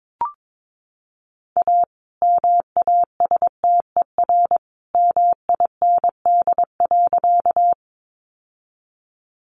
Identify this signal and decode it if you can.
Morse code, heard as a keyed tone, about 22 wpm, 700 Hz. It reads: A MASTER MIND.
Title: Sentences from Top 1300 Words 22wpm